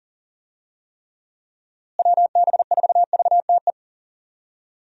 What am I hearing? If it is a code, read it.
Morse code, heard as a keyed tone, about 37 wpm, 700 Hz. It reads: WB4VTE
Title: USA Call Signs - 6 Characters - Repeated 3x - 40wpm